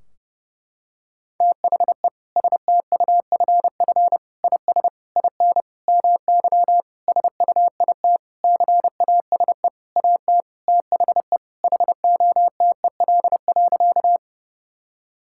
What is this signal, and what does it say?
Morse code, heard as a keyed tone, about 29 wpm, 700 Hz. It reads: THE STUFF IS IN MY SUIT CASE AT THE HOTEL.
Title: Sentences from Top 1600 Words 30wpm